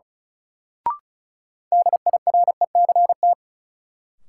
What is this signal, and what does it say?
Morse code, heard as a keyed tone, about 34 wpm, 700 Hz. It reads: DIRECT
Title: Sentences from Top 400 Words - New Word Drill - 35wpm